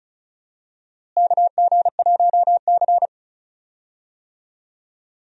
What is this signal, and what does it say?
Morse code, heard as a keyed tone, about 34 wpm, 700 Hz. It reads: KG1C
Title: USA Call Signs - 4 Characters - Repeated 3x - 35wpm